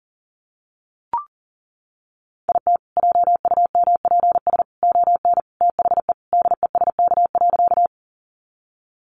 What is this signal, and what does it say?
Morse code, heard as a keyed tone, about 37 wpm, 700 Hz. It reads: IT JUMPS ON THE DESK.
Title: Additional Easy Phrases 40wpm